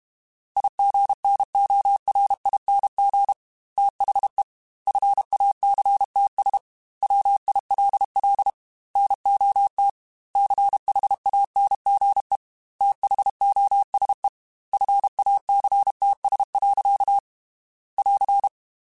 Morse code, heard as a keyed tone, about 32 wpm, 800 Hz. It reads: IGNORING THE FACTS WILL NOT CHANGE THOSE FACTS. <AR>
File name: Quotes / Quotes-mp3s / IgnoringFacts @30